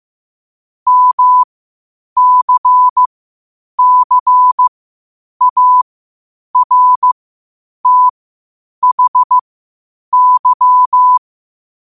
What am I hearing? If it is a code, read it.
Morse code, heard as a keyed tone, about 15 wpm, spaced as for 8 wpm, 1000 Hz. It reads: MCCARTHY
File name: qotdfiles / 8wpm